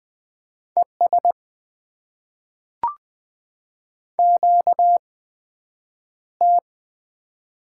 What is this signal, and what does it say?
Morse code, heard as a keyed tone, about 20 wpm, 700 Hz. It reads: E S Q T
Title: Top 20 QSO Elements - 8x Character Spacing - 20wpm